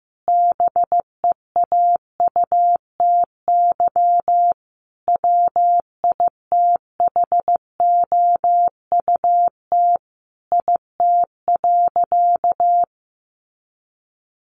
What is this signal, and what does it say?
Morse code, heard as a keyed tone, about 15 wpm, 700 Hz. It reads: BEAUTY WITHOUT IT.
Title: Sentences from Top 600 Words 15wpm